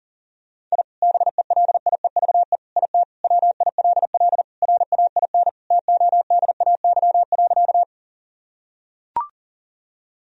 Morse code, heard as a keyed tone, about 40 wpm, 700 Hz. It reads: I BELIEVE IT WILL RAIN TODAY.
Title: Sentences from Top 1000 Words - Review Part 1 - 40wpm